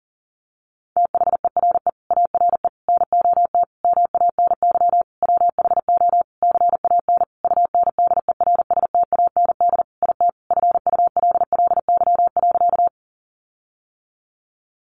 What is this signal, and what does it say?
Morse code, heard as a keyed tone, about 40 wpm, 700 Hz. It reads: THERE ARE NOT MANY WHO CAN UNDERSTAND IT FULLY.
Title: Sentences from Top 1300 Words 40wpm